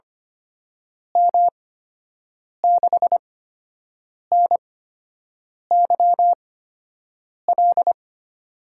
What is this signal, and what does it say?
Morse code, heard as a keyed tone, about 25 wpm, spaced as for 7 wpm, 700 Hz. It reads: M6NYL
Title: USA Call Signs - 6 Characters - 8x Character Spacing - 25wpm